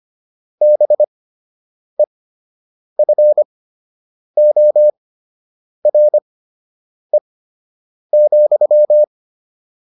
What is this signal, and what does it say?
Morse code, heard as a keyed tone, about 25 wpm, spaced as for 8 wpm, 600 Hz. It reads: BEFORE,